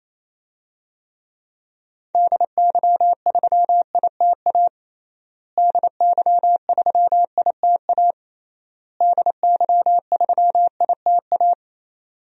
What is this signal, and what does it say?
Morse code, heard as a keyed tone, about 28 wpm, 700 Hz. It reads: DY3ITA DY3ITA DY3ITA
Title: DXCC-100 Call Signs - 6 Characters - Repeated 3x - 28wpm